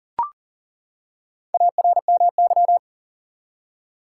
Morse code, heard as a keyed tone, about 40 wpm, 700 Hz. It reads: ARMY
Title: Four Letter Words - 40wpm